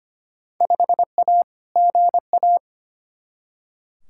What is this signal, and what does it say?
Morse code, heard as a keyed tone, about 25 wpm, 700 Hz. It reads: 5A GA
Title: Field Day Exchanges Encore - 25wpm